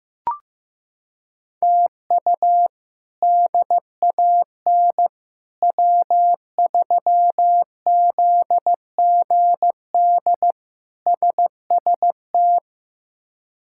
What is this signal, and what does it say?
Morse code, heard as a keyed tone, about 15 wpm, 700 Hz. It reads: TU DAN W3ZGD SST
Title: K1USN SST QSO Practice - 15wpm